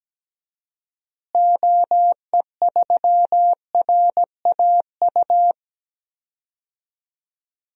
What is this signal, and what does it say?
Morse code, heard as a keyed tone, about 17 wpm, 700 Hz. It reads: OE3RAU
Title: DXCC-100 Call Signs - 6 Characters - Repeated 3x - 17wpm